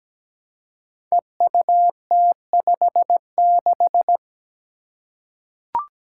Morse code, heard as a keyed tone, about 17 wpm, 700 Hz. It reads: EUT56
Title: Code Groups - 5 Letter:Numbers - 17wpm